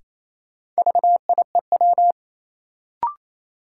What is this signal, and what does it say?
Morse code, heard as a keyed tone, about 28 wpm, 700 Hz. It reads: VIEW
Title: Sets of 5 Words - New Word Drill - 28wpm